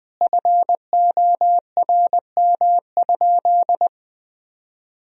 Morse code, heard as a keyed tone, about 20 wpm, 700 Hz. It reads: FORM?
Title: Sentences from Top 1000 Words - Review Part 1 - 20wpm